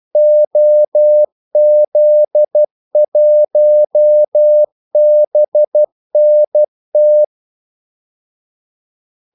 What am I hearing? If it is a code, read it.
Morse code, heard as a keyed tone, about 12 wpm, 600 Hz. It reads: OZ1BNT
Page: 2 ekstra ordmellemrum mellem kaldesignalerne